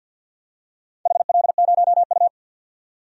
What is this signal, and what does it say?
Morse code, heard as a keyed tone, about 52 wpm, 700 Hz. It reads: HL0U
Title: DXCC-100 Call Signs - 4 Characters - 50wpm